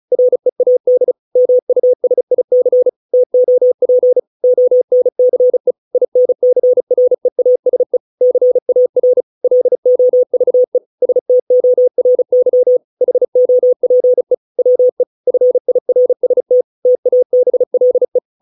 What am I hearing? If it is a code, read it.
Morse code, heard as a keyed tone, about 34 wpm, 500 Hz. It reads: READ MUSIC TOP ONCE INCREASE CAR LOVE STORY HOPE WE FIRST TABLE
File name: 632 words / 632 Words 35WPM